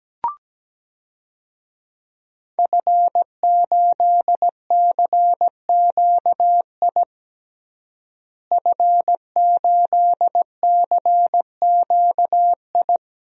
Morse code, heard as a keyed tone, about 17 wpm, 700 Hz. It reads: F8CQI F8CQI
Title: DXCC-100 Call Signs - 5 Characters - Repeated 3x - 17wpm